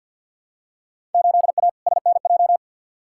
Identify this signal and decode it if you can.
Morse code, heard as a keyed tone, about 47 wpm, 700 Hz. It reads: 8A SNJ
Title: Field Day Exchanges Encore - 50wpm